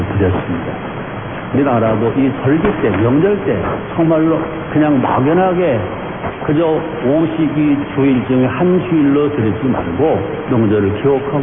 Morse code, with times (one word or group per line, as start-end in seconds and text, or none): none